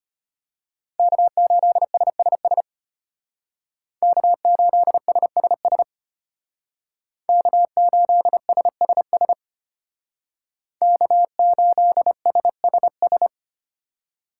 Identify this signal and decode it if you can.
Morse code, heard as a keyed tone, about 33 wpm, 700 Hz. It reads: K8SSS K8SSS K8SSS K8SSS